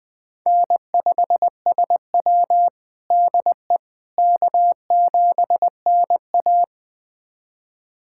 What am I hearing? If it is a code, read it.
Morse code, heard as a keyed tone, about 21 wpm, 700 Hz. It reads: N5SW DE K7NA
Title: USA Callsign DE Callsign - 4 Characters - 20wpm